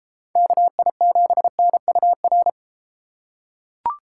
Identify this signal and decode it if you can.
Morse code, heard as a keyed tone, about 32 wpm, 700 Hz. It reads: KI7NUR